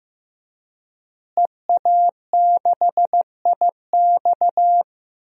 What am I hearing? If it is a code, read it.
Morse code, heard as a keyed tone, about 15 wpm, 700 Hz. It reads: EA6IX
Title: DXCC-100 Call Signs - 5 Characters - Repeated 3x - 15wpm